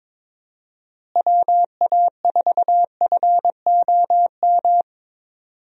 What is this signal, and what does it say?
Morse code, heard as a keyed tone, about 22 wpm, 700 Hz. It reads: WA4FOM
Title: USA Call Signs - 6 Characters - Repeated 3x - 22wpm